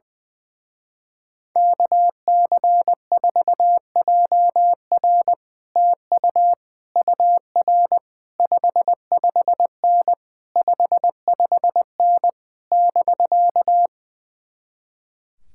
Morse code, heard as a keyed tone, about 20 wpm, 700 Hz. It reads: KC4JR TU UR 55N 55N <BK>
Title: Speed-Racing Common QSO Phrases - 20wpm